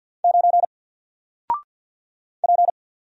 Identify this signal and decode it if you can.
Morse code, heard as a keyed tone, about 50 wpm, 700 Hz. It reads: 9 P
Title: Single Character (Letter:Number:Punctuation:Prosign) 50wpm